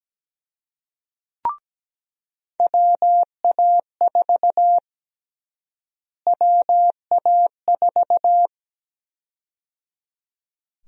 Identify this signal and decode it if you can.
Morse code, heard as a keyed tone, about 17 wpm, 700 Hz. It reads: WA4 WA4
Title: USA Call Sign Prefixes - Repeated 2x - 17wpm